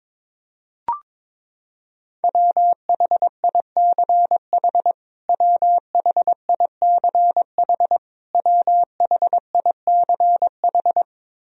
Morse code, heard as a keyed tone, about 22 wpm, 700 Hz. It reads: WHICH WHICH WHICH
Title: Top 100 Words Repeated 22wpm